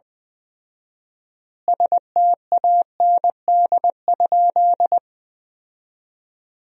Morse code, heard as a keyed tone, about 21 wpm, 700 Hz. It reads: STAND?
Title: 4x Word-Spacing - Sentences from Top 300 Words - 20wpm